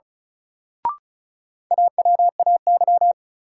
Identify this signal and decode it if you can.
Morse code, heard as a keyed tone, about 34 wpm, 700 Hz. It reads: AWAY
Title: Sets of 2 Words - New Word Drill - 35wpm